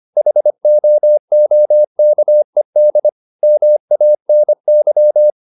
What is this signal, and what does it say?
Morse code, heard as a keyed tone, about 25 wpm, 600 Hz. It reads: HOOKED MANY
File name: the old man and the sea part 1 25wpm